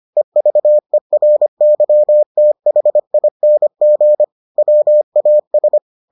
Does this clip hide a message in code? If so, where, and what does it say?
Yes: Morse code from 0.2 to 5.8 s, heard as a keyed tone, about 25 wpm, 600 Hz. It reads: EVERYTHING WAS